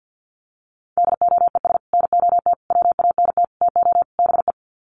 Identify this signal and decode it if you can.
Morse code, heard as a keyed tone, about 50 wpm, 700 Hz. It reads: DOES NOT WANT TO BE
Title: Sets of 5 Words Encore 50wpm